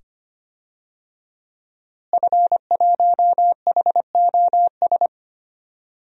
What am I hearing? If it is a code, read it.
Morse code, heard as a keyed tone, about 25 wpm, 700 Hz. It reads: F1HOS
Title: DXCC-100 Call Signs - 5 Characters - Repeated 2x - 25wpm